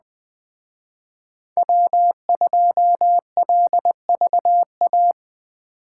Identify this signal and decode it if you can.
Morse code, heard as a keyed tone, about 20 wpm, 700 Hz. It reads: W2LVA
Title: Speed Racing - USA 1x3 Call Signs 17wpm